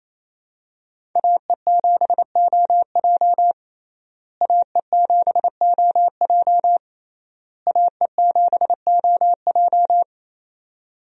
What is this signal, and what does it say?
Morse code, heard as a keyed tone, about 27 wpm, 700 Hz. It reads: AE7OJ AE7OJ AE7OJ
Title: USA Call Signs - 5 Characters - Repeated 3x - 28wpm